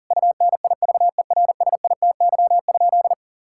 Morse code, heard as a keyed tone, about 40 wpm, 700 Hz. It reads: UNIVERSITY?
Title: Sentences from Top 1000 Words - Review Part 5 - 40wpm